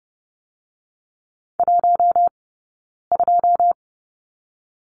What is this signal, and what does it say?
Morse code, heard as a keyed tone, about 30 wpm, 700 Hz. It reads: 1 2
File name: Lesson 17d - Letter u - Two Characters 30wpm